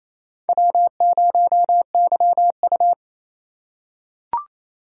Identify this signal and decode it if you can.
Morse code, heard as a keyed tone, about 28 wpm, 700 Hz. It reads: W0YU
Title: USA Call Signs - 4 Characters - 2x Character Spacing - 28wpm